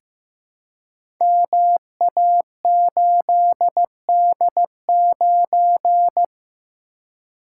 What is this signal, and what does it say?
Morse code, heard as a keyed tone, about 15 wpm, 700 Hz. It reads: MA8D9
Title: Code Groups - 5 Letter:Numbers - 15wpm